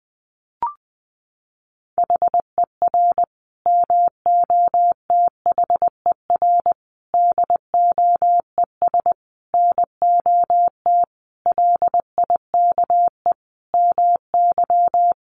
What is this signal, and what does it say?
Morse code, heard as a keyed tone, about 20 wpm, 700 Hz. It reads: HER MOTHER DOES NOT LIKE MY
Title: Sentences from Top 1400 Words 20wpm